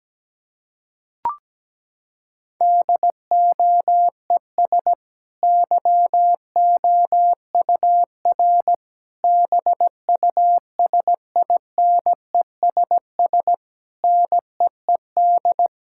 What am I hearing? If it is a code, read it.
Morse code, heard as a keyed tone, about 17 wpm, 700 Hz. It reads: DOES YOUR BUSINESS NEED
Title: Sentences from Top 1000 Words - Review Part 6 - 17wpm